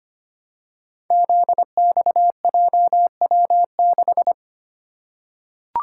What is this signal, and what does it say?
Morse code, heard as a keyed tone, about 25 wpm, 700 Hz. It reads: ZXJW6
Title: Code Groups - 5 Letter:Numbers - 25wpm